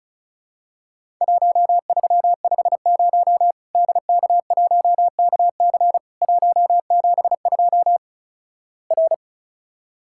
Morse code, heard as a keyed tone, about 36 wpm, 700 Hz. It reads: 1350 DK1KC 172 R
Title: Worked All Europe DX Contest - QTC Exchange - 35wpm